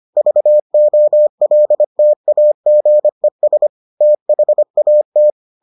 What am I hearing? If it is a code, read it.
Morse code, heard as a keyed tone, about 25 wpm, 600 Hz. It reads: VOLTAGES THAT